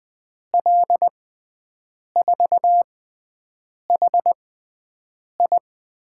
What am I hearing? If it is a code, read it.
Morse code, heard as a keyed tone, about 20 wpm, spaced as for 7 wpm, 700 Hz. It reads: L4HI
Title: USA Call Signs - 5 Characters - 6x Character Spacing - 20wpm